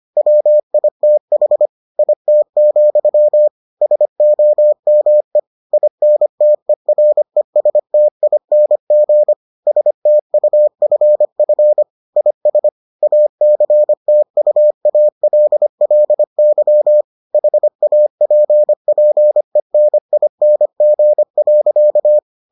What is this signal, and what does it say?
Morse code, heard as a keyed tone, about 25 wpm, 600 Hz. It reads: WITH IT, SOME INTERESTING STUFF IS ACTUALLY HAPPENING.